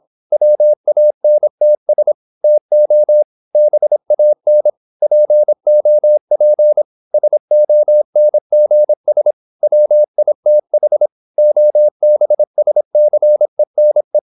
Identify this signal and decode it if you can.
Morse code, heard as a keyed tone, about 26 wpm, 600 Hz. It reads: WANTS TO BAN POP SONGS WITH OBSCENE
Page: Never 26wpm | CW med Gnister